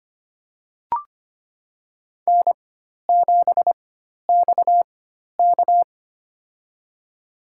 Morse code, heard as a keyed tone, about 25 wpm, spaced as for 12 wpm, 700 Hz. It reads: N7XK
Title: USA Call Signs - 4 Characters - 4x Character Spacing - 25wpm